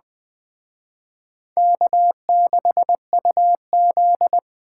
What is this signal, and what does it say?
Morse code, heard as a keyed tone, about 21 wpm, 700 Hz. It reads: K6UZ
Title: Speed Racing - USA 1x2 Call Signs 17wpm